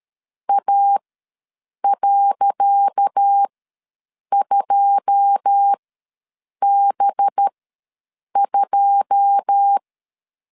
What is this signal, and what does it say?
Morse code, heard as a keyed tone, about 13 wpm, spaced as for 7 wpm, 800 Hz. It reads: A.2B2
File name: LEZIONI / 10 SIGNS / 10C DOT ws 12wpm 3548